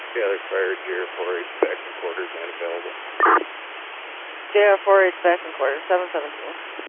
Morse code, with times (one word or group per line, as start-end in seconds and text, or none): none